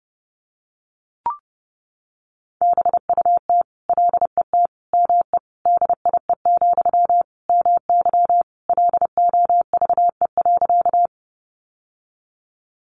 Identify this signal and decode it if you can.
Morse code, heard as a keyed tone, about 30 wpm, 700 Hz. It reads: BUT LET ME DIE, MY LOVE.
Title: Sentences from Top 1300 Words 30wpm